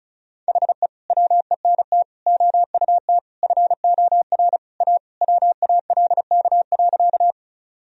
Morse code, heard as a keyed tone, about 35 wpm, 700 Hz. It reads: HE WENT OUT FOR A WALK.
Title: Sentences from Top 1000 Words - Review Part 1 - 35wpm